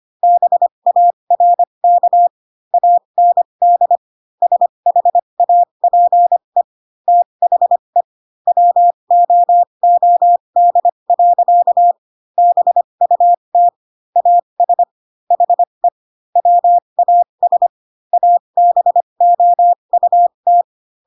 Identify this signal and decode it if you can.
Morse code, heard as a keyed tone, about 25 wpm, spaced as for 22 wpm, 700 Hz. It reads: BARK AND SHAPE THE WOOD. BUT AS HE WAS ABOUT